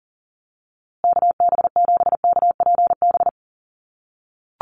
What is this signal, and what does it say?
Morse code, heard as a keyed tone, about 40 wpm, 700 Hz. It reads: KB7KPB
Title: USA Call Signs Repeated 3x 40wpm